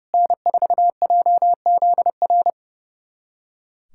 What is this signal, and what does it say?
Morse code, heard as a keyed tone, about 30 wpm, 700 Hz. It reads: N4JZR
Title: USA Callsign DE Callsign - 5 Characters - 30wpm